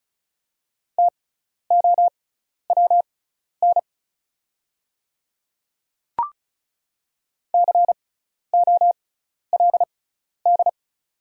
Morse code, heard as a keyed tone, about 36 wpm, spaced as for 12 wpm, 690 Hz. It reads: TOWN COLD
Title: 6x Character Spacing - Top 500 Words - 35wpm